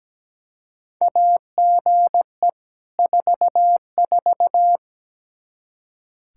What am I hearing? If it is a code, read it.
Morse code, heard as a keyed tone, about 17 wpm, 700 Hz. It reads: AGE 44
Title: Practice QSO - Exchange 3 - More Info - 17wpm